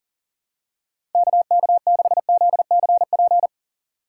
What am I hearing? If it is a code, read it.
Morse code, heard as a keyed tone, about 40 wpm, 700 Hz. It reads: KK6ZCP
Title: USA Call Signs - 6 Characters - 3x Character Spacing - 40wpm